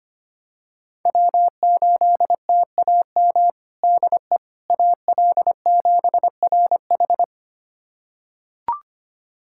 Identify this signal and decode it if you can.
Morse code, heard as a keyed tone, about 25 wpm, 700 Hz. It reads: W8TAM DE AL7RH
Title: USA Callsign DE Callsign - 5 Characters - 25wpm